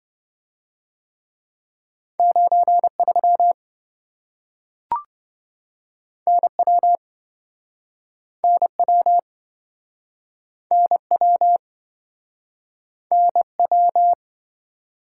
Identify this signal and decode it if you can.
Morse code, heard as a keyed tone, about 27 wpm, 700 Hz. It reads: 93 NW NW NW NW